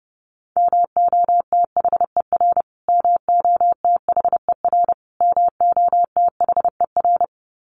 Morse code, heard as a keyed tone, about 30 wpm, 700 Hz. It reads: MOTHER MOTHER MOTHER
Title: Top 300 Words Repeated 30wpm